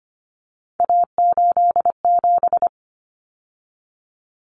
Morse code, heard as a keyed tone, about 25 wpm, 700 Hz. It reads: A87
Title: Common Cut Numbers - 3-Digit 25wpm